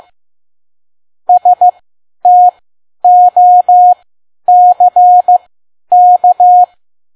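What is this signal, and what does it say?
Morse code, heard as a keyed tone, about 15 wpm, spaced as for 10 wpm, 720 Hz. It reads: STOCK
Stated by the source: Morse code practice at 10 WPM.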